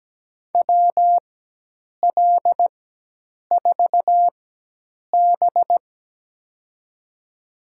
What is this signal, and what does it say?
Morse code, heard as a keyed tone, about 17 wpm, spaced as for 8 wpm, 700 Hz. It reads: WL4B
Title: USA Call Signs - 4 Characters - 4x Character Spacing - 17wpm